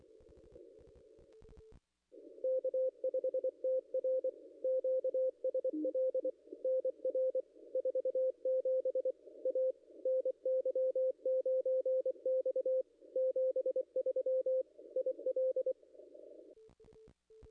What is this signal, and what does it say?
Morse code, heard as a keyed tone, about 24 wpm, 520 Hz. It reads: K5TR QSL NR 47 A NY9X 73 IL
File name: contests / ss-cw-k5tr-op-n5rz -0100